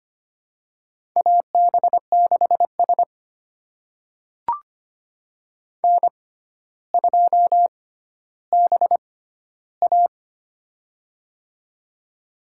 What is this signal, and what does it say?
Morse code, heard as a keyed tone, about 25 wpm, spaced as for 13 wpm, 700 Hz. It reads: AB6S N2BA
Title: USA Call Signs - 4 Characters - 6x Character Spacing - 25wpm